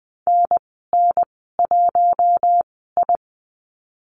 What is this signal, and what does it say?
Morse code, heard as a keyed tone, about 20 wpm, spaced as for 15 wpm, 700 Hz. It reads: NN1I
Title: 055 Lesson 10g - Letter r - Call Signs 20wpm